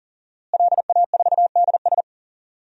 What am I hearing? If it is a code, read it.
Morse code, heard as a keyed tone, about 40 wpm, 700 Hz. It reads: LA4DS
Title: DXCC-100 Call Signs - 5 Characters - Repeated 3x - 40wpm